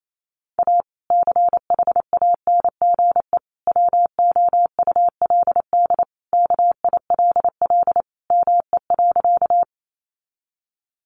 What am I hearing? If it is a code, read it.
Morse code, heard as a keyed tone, about 28 wpm, 700 Hz. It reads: A CHANGE WOULD KILL ME.
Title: Sentences from Top 1400 Words 28wpm